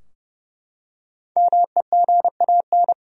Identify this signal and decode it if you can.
Morse code, heard as a keyed tone, about 32 wpm, 700 Hz. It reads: MEGAN
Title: USA Common Names - Max 5 Letters - 30wpm